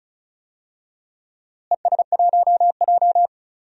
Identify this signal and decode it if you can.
Morse code, heard as a keyed tone, about 36 wpm, 700 Hz. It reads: ES1J
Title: DXCC-100 Call Signs - 4 Characters - Repeated 2x - 35wpm